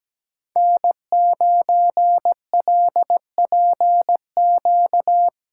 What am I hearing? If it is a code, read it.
Morse code, heard as a keyed tone, about 17 wpm, 700 Hz. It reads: N9LPQ
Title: USA Callsign DE Callsign 17wpm